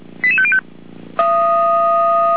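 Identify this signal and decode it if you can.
Five-tone selective call (ZVEI), heard as quick runs of five tones, tones of about 70 ms. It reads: 88505
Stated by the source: ZVEI five-tone selective call 88352 received by SDR.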